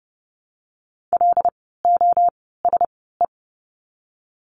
Morse code, heard as a keyed tone, about 30 wpm, spaced as for 17 wpm, 700 Hz. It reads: LOSE
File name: Lesson 16f - Letter c - Words 30wpm